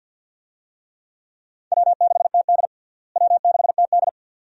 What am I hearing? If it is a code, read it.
Morse code, heard as a keyed tone, about 46 wpm, 700 Hz. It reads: W6TD W6TD
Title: USA Call Signs - 4 Characters - Repeated 2x - 50wpm